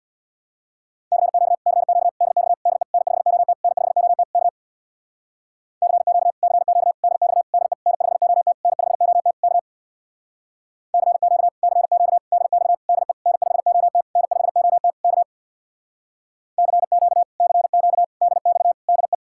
Morse code, heard as a keyed tone, about 61 wpm, 700 Hz. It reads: CQ CQ DX DE N5OT N5OT K CQ CQ DX DE N5OT N5OT K CQ CQ DX DE N5OT N5OT K CQ CQ DX DE
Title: Speed-Racing Common QSO Phrases - 50wpm